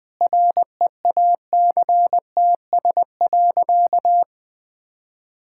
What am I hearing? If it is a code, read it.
Morse code, heard as a keyed tone, about 19 wpm, 700 Hz. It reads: REACTS.
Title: Sentences with 8-Word Phrases - Spoken Beginning - 20wpm